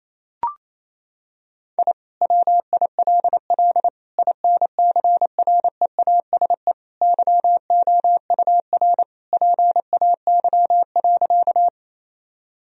Morse code, heard as a keyed tone, about 28 wpm, 700 Hz. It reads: I WILL INCREASE YOUR PAY.
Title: Sentences from Top 1000 Words - Review Part 4 - 28wpm